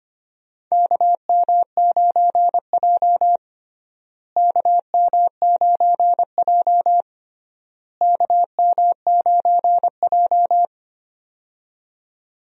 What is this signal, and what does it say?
Morse code, heard as a keyed tone, about 25 wpm, 700 Hz. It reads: KM9J KM9J KM9J
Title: USA Call Signs - 4 Characters - Repeated 3x - 25wpm